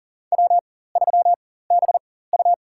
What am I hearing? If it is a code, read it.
Morse code, heard as a keyed tone, about 40 wpm, spaced as for 19 wpm, 700 Hz. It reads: W3BU
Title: USA Call Signs - 4 Characters - 4x Character Spacing - 40wpm